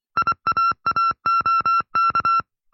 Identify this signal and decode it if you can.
Morse code, heard as a keyed tone, about 24 wpm, 1350 Hz. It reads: IAAOK
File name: FreeFiles / CW-practice / 20WPM-side1